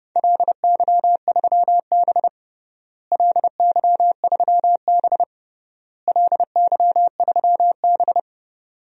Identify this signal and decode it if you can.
Morse code, heard as a keyed tone, about 31 wpm, 700 Hz. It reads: LY3B LY3B LY3B
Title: DXCC-100 Call Signs - 4 Characters - Repeated 3x - 30wpm